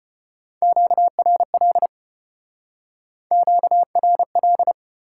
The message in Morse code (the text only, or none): QRL QRL